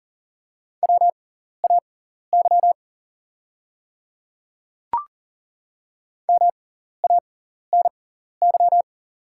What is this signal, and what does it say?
Morse code, heard as a keyed tone, about 38 wpm, spaced as for 14 wpm, 690 Hz. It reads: WAY MANY
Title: 6x Character Spacing - Top 100 Words - 40wpm